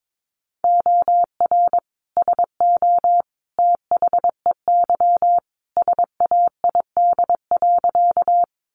Morse code, heard as a keyed tone, about 22 wpm, 700 Hz. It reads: OR SO THEY SAID.
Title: Sentences from Top 100 Words 22wpm